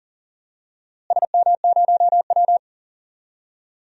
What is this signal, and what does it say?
Morse code, heard as a keyed tone, about 40 wpm, 700 Hz. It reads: SM0W
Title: DXCC-100 Call Signs - 4 Characters - Repeated 3x - 40wpm